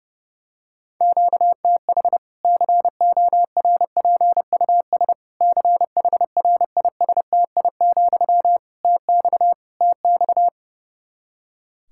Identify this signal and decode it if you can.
Morse code, heard as a keyed tone, about 30 wpm, 700 Hz. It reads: QTH CORPUS CHRISTI, TX TX
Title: K1USN SST QSO Practice 30wpm